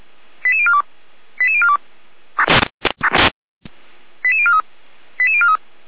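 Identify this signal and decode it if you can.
Five-tone selective call (ZVEI), heard as quick runs of five tones, tones of about 70 ms. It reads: 88042 88042 88043 88043
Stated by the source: Received by SDR; ZVEI five-tone selective call.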